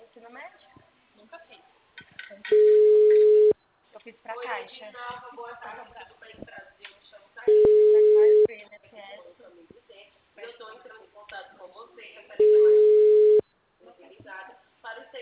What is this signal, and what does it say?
Ringback tone (Europe, 425 Hz), with speech.
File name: out- -2003- - - .336